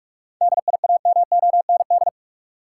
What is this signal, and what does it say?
Morse code, heard as a keyed tone, about 46 wpm, 700 Hz. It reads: DIAMOND
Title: Seven Letter Words - 45wpm